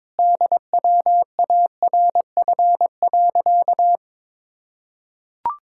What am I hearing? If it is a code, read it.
Morse code, heard as a keyed tone, about 22 wpm, 700 Hz. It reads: DWARF.
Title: Pangram Challenge v2 - 22wpm